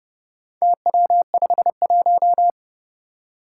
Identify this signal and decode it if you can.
Morse code, heard as a keyed tone, about 30 wpm, 700 Hz. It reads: TW51
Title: Four Challenging Letter-Number Combinations 30wpm